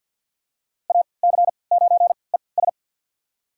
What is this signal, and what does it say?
Morse code, heard as a keyed tone, about 48 wpm, spaced as for 28 wpm, 700 Hz. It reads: AC9ES
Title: USA Call Signs - 5 Characters - 3x Character Spacing - 50wpm